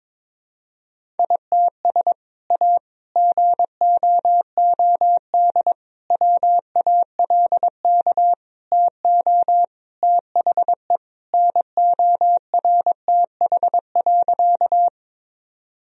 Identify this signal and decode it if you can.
Morse code, heard as a keyed tone, about 22 wpm, 700 Hz. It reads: ITS A GOOD WALK TO THE NORTH.
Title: Sentences from Top 1000 Words - Review Part 1 - 22wpm